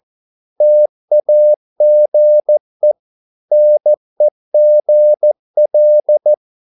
Morse code, heard as a keyed tone, about 14 wpm, 600 Hz. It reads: TAGE NEGL
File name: Korte ord DK 14wpm